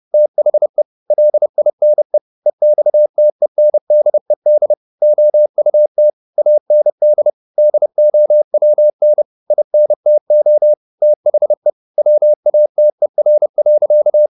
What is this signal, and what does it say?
Morse code, heard as a keyed tone, about 30 wpm, 600 Hz. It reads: THE LINE EXTENDED OUT AND DOWN INTO THE WATER.
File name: the old man and the sea / the old man and the sea 1 30wpm